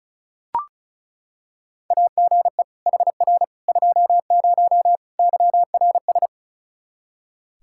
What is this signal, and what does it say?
Morse code, heard as a keyed tone, about 36 wpm, 700 Hz. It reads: AGE HR 20 YRS
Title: K1USN SST QSO Practice 35wpm